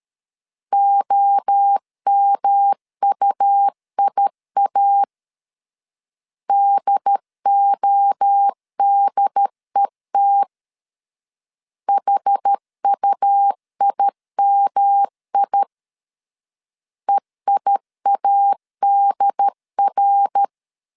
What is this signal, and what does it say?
Morse code, heard as a keyed tone, about 13 wpm, 800 Hz. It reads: OMUIA DODET HUIMI EIADR
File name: CW / DET / 1 3 1500C S C 12wpm 2750